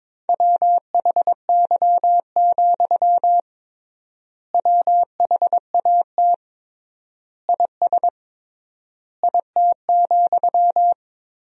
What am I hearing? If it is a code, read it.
Morse code, heard as a keyed tone, about 21 wpm, 700 Hz. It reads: WHY, WHAT IS IT,
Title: 3x Word-Spacing - Sentences from Top 200 Words - 22wpm